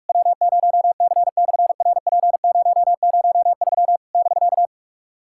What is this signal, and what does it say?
Morse code, heard as a keyed tone, about 46 wpm, 700 Hz. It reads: W0C/RP003 <BK>
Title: SOTA QSO Practice 45wpm